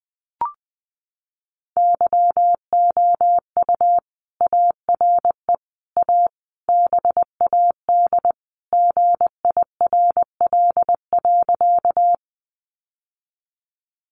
Morse code, heard as a keyed tone, about 20 wpm, 700 Hz. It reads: YOU ARE A BAD GIRL.
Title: Sentences from Top 700 Words 20wpm